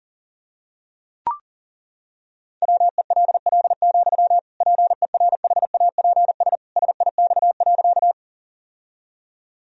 Morse code, heard as a keyed tone, about 40 wpm, 700 Hz. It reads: WELL, PERHAPS SIX.
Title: Sentences from Top 1000 Words - Review Part 2 - 40wpm